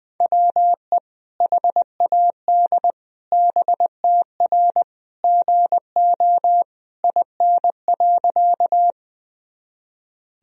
Morse code, heard as a keyed tone, about 21 wpm, 700 Hz. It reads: WE HAD BTR GO IN.
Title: Sentences from Top 400 Words and Abbreviations 20wpm